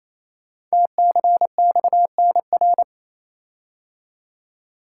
Morse code, heard as a keyed tone, about 27 wpm, 700 Hz. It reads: TCXNR
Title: Code Groups - 5 Letters - 28wpm